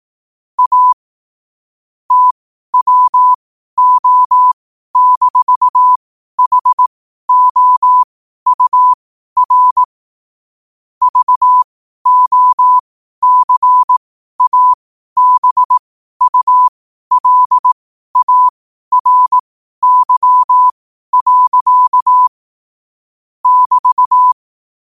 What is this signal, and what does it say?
Morse code, heard as a keyed tone, about 18 wpm, spaced as for 12 wpm, 1000 Hz. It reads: A TWO-HOUR VOCABULARY. <BT>
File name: qotdfiles / 12wpm